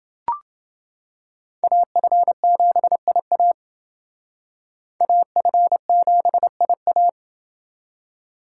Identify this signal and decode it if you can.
Morse code, heard as a keyed tone, about 29 wpm, 700 Hz. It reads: AF7IA AF7IA